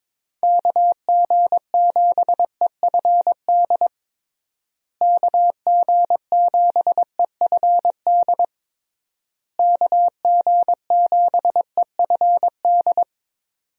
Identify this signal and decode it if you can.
Morse code, heard as a keyed tone, about 22 wpm, 700 Hz. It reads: KG7EFD KG7EFD KG7EFD
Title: USA Call Signs - 6 Characters - Repeated 3x - 22wpm